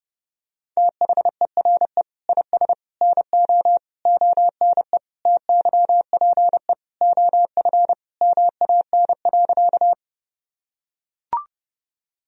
Morse code, heard as a keyed tone, about 32 wpm, 700 Hz. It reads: THERE IS NO ONE TYPE OF MAN.
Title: Sentences from Top 1000 Words - Review Part 3 - 30wpm